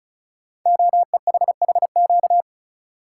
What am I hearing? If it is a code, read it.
Morse code, heard as a keyed tone, about 36 wpm, 700 Hz. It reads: OEHHQ
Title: Code Groups - 5 Letters - 35wpm